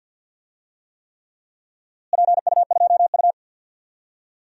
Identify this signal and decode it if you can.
Morse code, heard as a keyed tone, about 51 wpm, 700 Hz. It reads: PU2U
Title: DXCC-100 Call Signs - 4 Characters - Repeated 3x - 50wpm